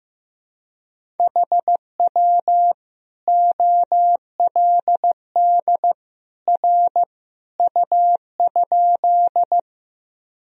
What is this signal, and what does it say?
Morse code, heard as a keyed tone, about 15 wpm, 700 Hz. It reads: HW OLD R U?
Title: Sentences from Top 200 Words and Abbreviations 15wpm